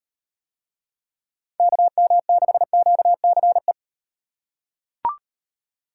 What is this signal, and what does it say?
Morse code, heard as a keyed tone, about 38 wpm, 700 Hz. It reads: KM6QCE